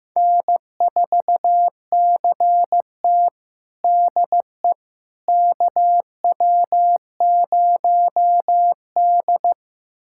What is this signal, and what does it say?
Morse code, heard as a keyed tone, about 15 wpm, 700 Hz. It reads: N4CT DE KW0D
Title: USA Callsign DE Callsign - 4 Characters - 15wpm